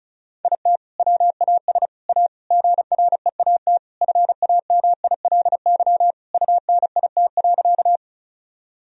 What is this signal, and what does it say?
Morse code, heard as a keyed tone, about 36 wpm, 700 Hz. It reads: IT WAS A GREAT FAMILY UNIT.
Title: Sentences from Top 1000 Words - Review Part 2 - 35wpm